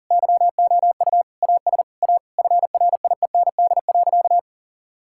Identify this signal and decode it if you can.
Morse code, heard as a keyed tone, about 40 wpm, 700 Hz. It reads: YOU AS A FRIEND.
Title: Sentences from Top 1000 Words - Review Part 6 - 40wpm